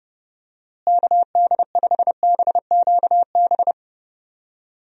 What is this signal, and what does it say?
Morse code, heard as a keyed tone, about 32 wpm, 700 Hz. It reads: KD5BQB
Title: USA Call Signs - 6 Characters - Repeated 2x - 30wpm